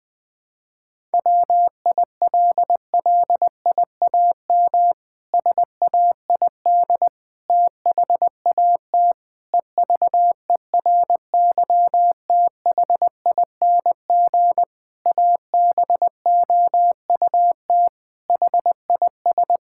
Morse code, heard as a keyed tone, about 21 wpm, 700 Hz. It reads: WILLIAM SAID THAT EVERYTHING ABOUT HIS
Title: Pangram Challenge v2 - 20wpm